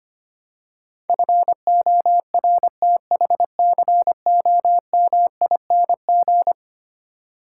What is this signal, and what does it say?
Morse code, heard as a keyed tone, about 25 wpm, 700 Hz. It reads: FORTHCOMING
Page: morse-code-ninja-practice-25wpm